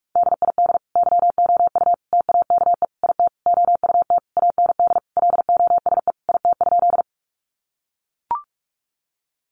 Morse code, heard as a keyed tone, about 44 wpm, 700 Hz. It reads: DID YOU TAKE IT OUT AND LOSE IT?
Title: Sentences from Top 1100 Words 45wpm